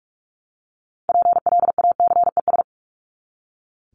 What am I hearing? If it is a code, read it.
Morse code, heard as a keyed tone, about 44 wpm, 700 Hz. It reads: PLACES
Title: ES:S Suffix - Top 100 Words with Base Words 45wpm